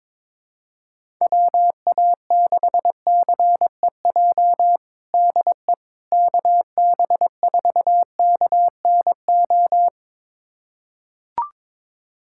Morse code, heard as a keyed tone, about 22 wpm, 700 Hz. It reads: WA6CEJ DE KB4KNO
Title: USA Callsign DE Callsign - 6 Characters - 22wpm